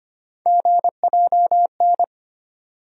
Morse code, heard as a keyed tone, about 25 wpm, 700 Hz. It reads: GJN
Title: USA Call Sign Postfixes - 3 Letters 25wpm